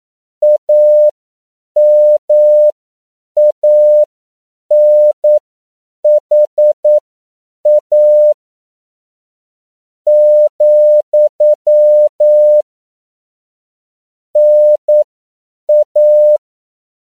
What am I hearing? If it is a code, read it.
Morse code, heard as a keyed tone, about 9 wpm, spaced as for 7 wpm, 600 Hz. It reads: AMANHA , NA